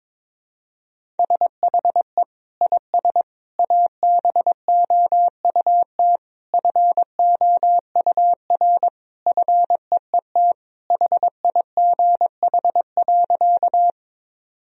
Morse code, heard as a keyed tone, about 22 wpm, 700 Hz. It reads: SHE IS ABOUT FOUR FEET HIGH.
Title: Sentences from Top 1000 Words - Review Part 1 - 22wpm